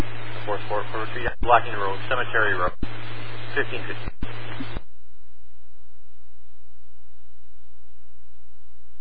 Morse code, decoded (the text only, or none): none